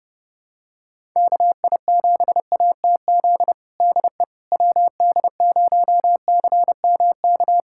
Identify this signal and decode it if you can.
Morse code, heard as a keyed tone, about 30 wpm, 700 Hz. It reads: KI7ATZ DE WD0CMK
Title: USA Callsign DE Callsign - 6 Characters - 30wpm